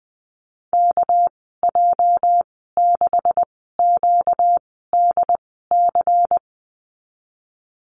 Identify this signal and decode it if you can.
Morse code, heard as a keyed tone, about 20 wpm, spaced as for 14 wpm, 700 Hz. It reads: KJ6QDC
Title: 245 Lesson 39g - Letter z - Call Signs 20wpm